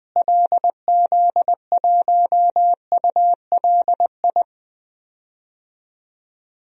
Morse code, heard as a keyed tone, about 20 wpm, 700 Hz. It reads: LZ1ULI
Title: DXCC-100 Call Signs - 6 Characters - Repeated 2x - 20wpm